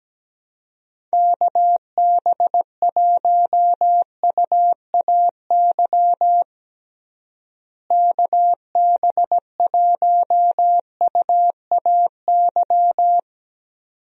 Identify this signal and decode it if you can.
Morse code, heard as a keyed tone, about 17 wpm, 700 Hz. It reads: KB1UAY KB1UAY
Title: USA Call Signs - 6 Characters - Repeated 2x - 17wpm